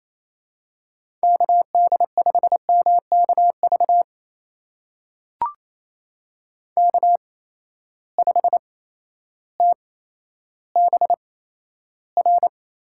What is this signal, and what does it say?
Morse code, heard as a keyed tone, about 27 wpm, spaced as for 13 wpm, 700 Hz. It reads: KD5MKV K5TBR
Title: USA Call Signs - 8x Character Spacing - 28wpm